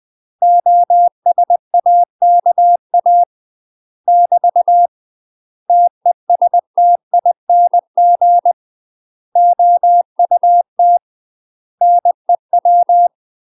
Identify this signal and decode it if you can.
Morse code, heard as a keyed tone, about 20 wpm, 700 Hz. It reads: OSAKA <BT> TESTING OUT NEW
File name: UK MASTER QSOs TRAINING / 20WPM / UK MASTER QSOs 0023